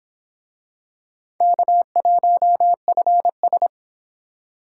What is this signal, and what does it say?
Morse code, heard as a keyed tone, about 25 wpm, 700 Hz. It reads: K1FS